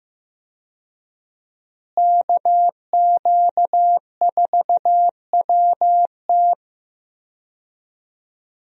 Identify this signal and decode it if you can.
Morse code, heard as a keyed tone, about 15 wpm, 700 Hz. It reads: KQ4WT
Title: USA Call Signs - 5 Characters - Repeated 3x - 15wpm